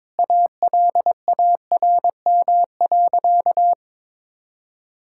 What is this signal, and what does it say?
Morse code, heard as a keyed tone, about 22 wpm, 700 Hz. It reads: ALARM.
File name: practice / 22wpm / Sentences with 8-Word Phrases - 22wpm